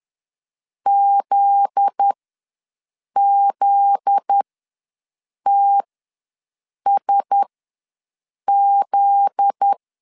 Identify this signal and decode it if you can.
Morse code, heard as a keyed tone, about 10 wpm, spaced as for 6 wpm, 800 Hz. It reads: ZZTSZ
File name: LEZIONI / 7 GF9Z / 7 GF9Z ws 10wpm 3919